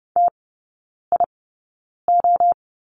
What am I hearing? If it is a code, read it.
Morse code, heard as a keyed tone, about 30 wpm, spaced as for 9 wpm, 690 Hz. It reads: TIO
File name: Lesson 17e - Letter u - Three Characters 30wpm